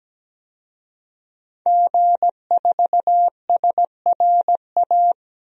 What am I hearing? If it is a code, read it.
Morse code, heard as a keyed tone, about 17 wpm, 700 Hz. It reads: G4SRA
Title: DXCC-100 Call Signs - 5 Characters - 17wpm